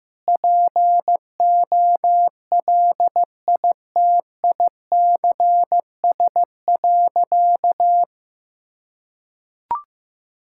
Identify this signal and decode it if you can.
Morse code, heard as a keyed tone, about 15 wpm, 700 Hz. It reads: POLITICS.
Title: Sentences from Top 1000 Words - Review Part 6 - 15wpm